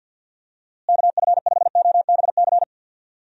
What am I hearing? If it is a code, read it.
Morse code, heard as a keyed tone, about 48 wpm, 700 Hz. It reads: KF5OBC
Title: USA Call Signs - 6 Characters - Repeated 2x - 50wpm